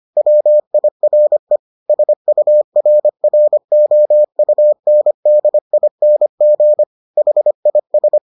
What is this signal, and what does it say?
Morse code, heard as a keyed tone, about 25 wpm, 600 Hz. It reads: WIRE SURROUNDING HIS